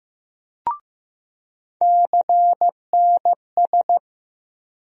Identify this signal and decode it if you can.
Morse code, heard as a keyed tone, about 15 wpm, 700 Hz. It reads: CNS
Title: USA Call Sign Postfixes - 3 Letters 15wpm